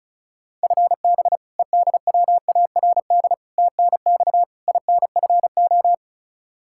Morse code, heard as a keyed tone, about 34 wpm, 700 Hz. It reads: FB EDWARD TNX INFO
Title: Practice QSO - Exchange 3 - More Info - 35wpm